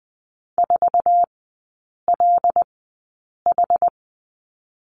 Morse code, heard as a keyed tone, about 20 wpm, spaced as for 8 wpm, 700 Hz. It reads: 4LH
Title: 100 Lesson 17e - Letter u - Three Characters 20wpm